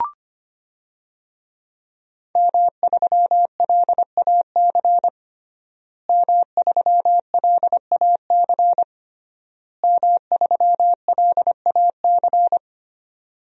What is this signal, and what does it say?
Morse code, heard as a keyed tone, about 25 wpm, 700 Hz. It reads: M3LAC M3LAC M3LAC
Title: DXCC-100 Call Signs - 5 Characters - Repeated 3x - 25wpm